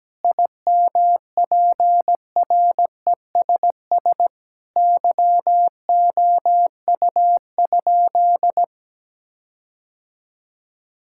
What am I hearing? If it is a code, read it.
Morse code, heard as a keyed tone, about 17 wpm, 700 Hz. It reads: IMPRESS YOU?
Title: Sentences from Top 1600 Words 17wpm